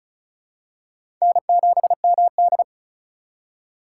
Morse code, heard as a keyed tone, about 36 wpm, 700 Hz. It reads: N7MD
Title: USA Call Signs - 4 Characters - 2x Character Spacing - 35wpm